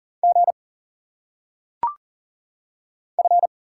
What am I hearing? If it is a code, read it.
Morse code, heard as a keyed tone, about 40 wpm, 700 Hz. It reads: G F
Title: Single Letters - Phonetic Pronunciation 40wpm